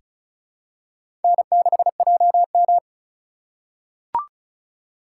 Morse code, heard as a keyed tone, about 34 wpm, 700 Hz. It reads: N6JM
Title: USA Call Signs - 4 Characters - 2x Character Spacing - 35wpm